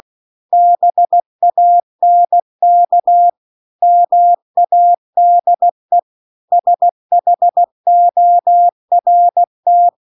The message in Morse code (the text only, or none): BANK MADE SHORT